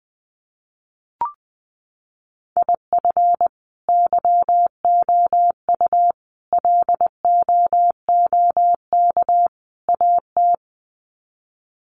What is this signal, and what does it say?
Morse code, heard as a keyed tone, about 20 wpm, 700 Hz. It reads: IF YOU LOOK AT
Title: Sets of 4 Words 20wpm